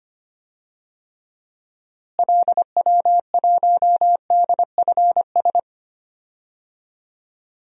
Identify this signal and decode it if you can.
Morse code, heard as a keyed tone, about 25 wpm, 700 Hz. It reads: LW1DFS
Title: DXCC-100 Call Signs - 6 Characters - Repeated 2x - 25wpm